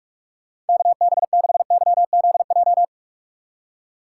Morse code, heard as a keyed tone, about 60 wpm, spaced as for 52 wpm, 700 Hz. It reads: KB6YZJ
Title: Speed Racing - USA 2x3 Call Signs 45wpm